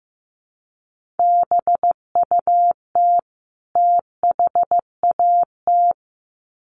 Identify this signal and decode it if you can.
Morse code, heard as a keyed tone, about 15 wpm, 700 Hz. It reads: BUT THAT
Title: Sets of 2 Words 15wpm